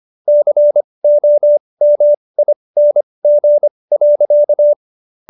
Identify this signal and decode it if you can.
Morse code, heard as a keyed tone, about 25 wpm, spaced as for 20 wpm, 590 Hz. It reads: COMING.